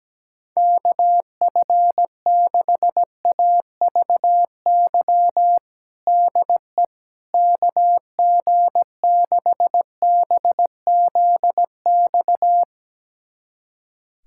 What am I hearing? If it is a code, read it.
Morse code, heard as a keyed tone, about 17 wpm, 700 Hz. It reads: KF6AVY DE KG6BZX
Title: USA Callsign DE Callsign - 6 Characters - 17wpm